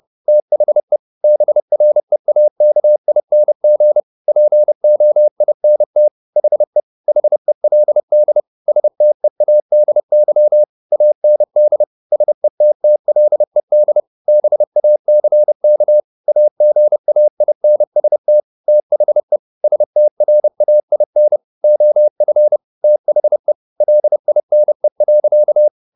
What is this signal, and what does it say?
Morse code, heard as a keyed tone, about 30 wpm, 600 Hz. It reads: THE BREAKING POINT HE HELD STEADY AND SETTLED BACK AGAINST THE STRAIN OF THE LINE.
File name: the old man and the sea / the old man and the sea 1 30wpm